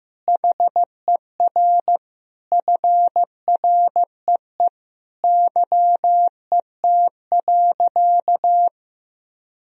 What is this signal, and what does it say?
Morse code, heard as a keyed tone, about 15 wpm, 700 Hz. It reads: HER FREE YET.
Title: Sentences from Top 1000 Words - Review Part 1 - 15wpm